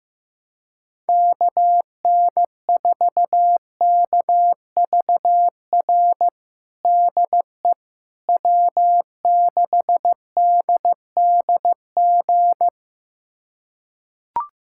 Morse code, heard as a keyed tone, about 16 wpm, 700 Hz. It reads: KN4KVR DE W6DDG
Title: USA Callsign DE Callsign 15wpm